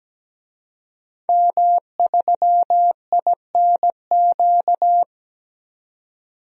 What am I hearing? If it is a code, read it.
Morse code, heard as a keyed tone, about 17 wpm, 700 Hz. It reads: M3INQ
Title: DXCC-100 Call Signs - 5 Characters - 17wpm